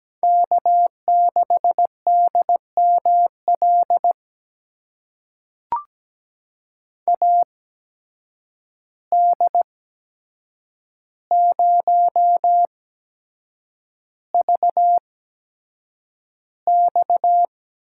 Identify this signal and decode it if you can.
Morse code, heard as a keyed tone, about 17 wpm, spaced as for 7 wpm, 700 Hz. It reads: K6DML AD0VX
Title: USA Call Signs - 5 Characters - 8x Character Spacing - 17wpm